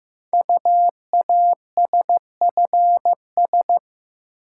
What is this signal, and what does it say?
Morse code, heard as a keyed tone, about 15 wpm, 700 Hz. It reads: UASFS
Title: Code Groups - 5 Letters - 15wpm